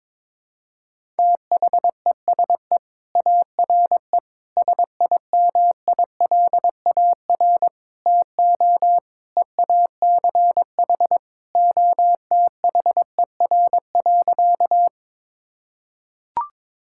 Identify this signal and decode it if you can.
Morse code, heard as a keyed tone, about 22 wpm, 700 Hz. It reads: THESE ARE SIMILAR TO EACH OTHER.
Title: Sentences from Top 1000 Words - Review Part 5 - 22wpm